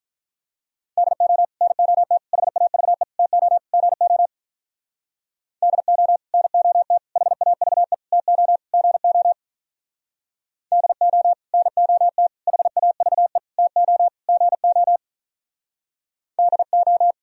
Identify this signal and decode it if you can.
Morse code, heard as a keyed tone, about 45 wpm, 700 Hz. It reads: DO NOT HAVE TO GO DO NOT HAVE TO GO DO NOT HAVE TO GO DO